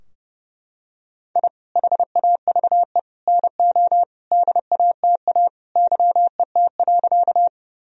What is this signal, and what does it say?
Morse code, heard as a keyed tone, about 30 wpm, 700 Hz. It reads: I HAVE NO DATA YET.
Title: Sentences from Top 1000 Words - Review Part 3 - 30wpm